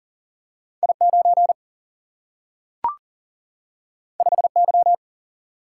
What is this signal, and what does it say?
Morse code, heard as a keyed tone, about 40 wpm, 700 Hz. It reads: I9 5Y
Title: Single-Letter Single-Number 40wpm